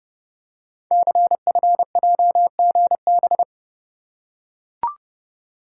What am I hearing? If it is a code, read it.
Morse code, heard as a keyed tone, about 29 wpm, 700 Hz. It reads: CFJGB
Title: Code Groups - 5 Letters - 30wpm